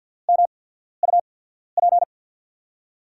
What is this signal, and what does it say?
Morse code, heard as a keyed tone, about 46 wpm, spaced as for 13 wpm, 700 Hz. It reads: MUP
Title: USA Call Signs - 5 Characters - 8x Character Spacing - 50wpm